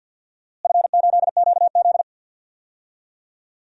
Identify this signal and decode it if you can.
Morse code, heard as a keyed tone, about 52 wpm, 700 Hz. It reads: F8QZ
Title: DXCC-100 Call Signs - 4 Characters - Repeated 2x - 50wpm